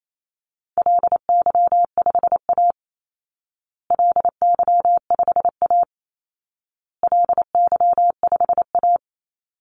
Morse code, heard as a keyed tone, about 28 wpm, 700 Hz. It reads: LY5A LY5A LY5A
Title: Call Signs Repeated 2x - 28wpm